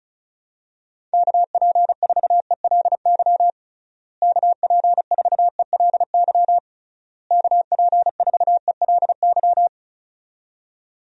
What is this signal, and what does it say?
Morse code, heard as a keyed tone, about 34 wpm, 700 Hz. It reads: KP4ELY KP4ELY KP4ELY
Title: USA Call Signs - 6 Characters - Repeated 3x - 35wpm